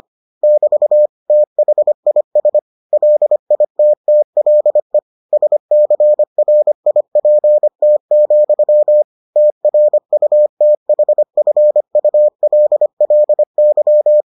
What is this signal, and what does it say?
Morse code, heard as a keyed tone, about 25 wpm, 600 Hz. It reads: <BT> THIS LITTLE SCRIPT, TRUTHFULLY